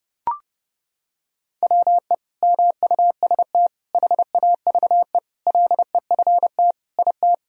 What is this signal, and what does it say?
Morse code, heard as a keyed tone, about 32 wpm, 700 Hz. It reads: WE MUST HAVE LEFT IT
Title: Sentences from Top 1000 Words - Review Part 2 - 30wpm